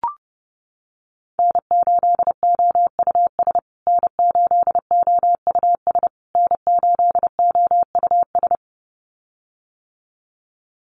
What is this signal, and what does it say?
Morse code, heard as a keyed tone, about 30 wpm, 700 Hz. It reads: N8OUS N8OUS N8OUS
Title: USA Call Signs Repeated 3x 30wpm